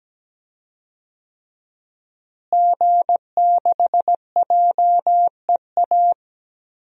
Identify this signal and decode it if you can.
Morse code, heard as a keyed tone, about 17 wpm, 700 Hz. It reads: G6JEA
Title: DXCC-100 Call Signs - 5 Characters - Repeated 3x - 17wpm